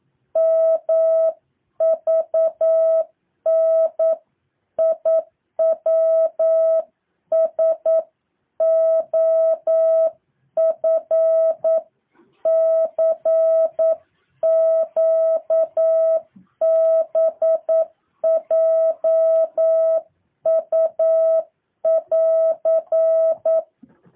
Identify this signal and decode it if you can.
Morse code, heard as a keyed tone, about 9 wpm, 650 Hz. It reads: MVNIWSOFCQBJU<AR>
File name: jh8cbh / 2023cw